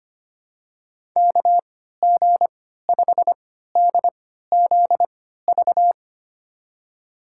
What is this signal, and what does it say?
Morse code, heard as a keyed tone, about 25 wpm, spaced as for 14 wpm, 700 Hz. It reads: KG5DZV
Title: USA Call Signs - 6 Characters - 3x Character Spacing - 25wpm